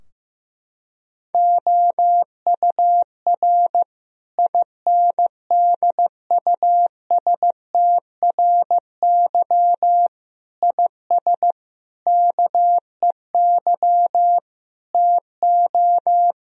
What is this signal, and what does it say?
Morse code, heard as a keyed tone, about 15 wpm, 700 Hz. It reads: OUR INDUSTRY IS KEY TO
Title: Sentences from Top 1000 Words - Review Part 4 - 15wpm